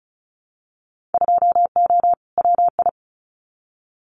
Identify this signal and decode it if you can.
Morse code, heard as a keyed tone, about 34 wpm, 700 Hz. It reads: 2O WI
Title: Winter Field Day Exchanges 35wpm